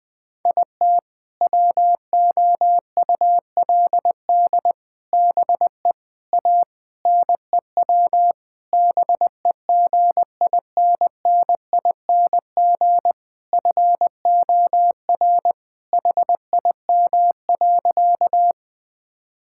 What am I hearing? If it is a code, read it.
Morse code, heard as a keyed tone, about 20 wpm, 700 Hz. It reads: IT WOULD BE A NEW BEGINNING FOR HIM.
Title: Sentences from Top 1000 Words - Review Part 5 - 20wpm